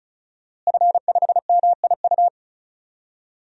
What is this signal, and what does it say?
Morse code, heard as a keyed tone, about 34 wpm, 700 Hz. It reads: F5MIU
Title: DXCC-100 Call Signs - 5 Characters - Repeated 2x - 35wpm